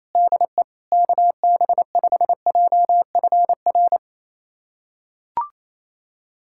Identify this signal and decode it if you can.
Morse code, heard as a keyed tone, about 28 wpm, 700 Hz. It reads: DE KB5JFR
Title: USA Callsign DE Callsign - 6 Characters - 28wpm